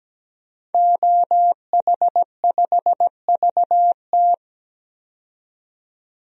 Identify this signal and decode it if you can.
Morse code, heard as a keyed tone, about 17 wpm, 700 Hz. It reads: OH5VT
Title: DXCC-100 Call Signs - 5 Characters - 17wpm